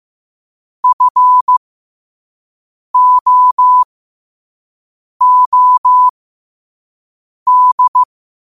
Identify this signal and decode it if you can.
Morse code, heard as a keyed tone, about 15 wpm, spaced as for 5 wpm, 1000 Hz. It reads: FOOD
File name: qotdfiles / 5wpm